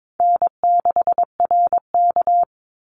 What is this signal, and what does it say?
Morse code, heard as a keyed tone, about 22 wpm, 700 Hz. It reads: N6RK
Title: Easy Call Signs - 22wpm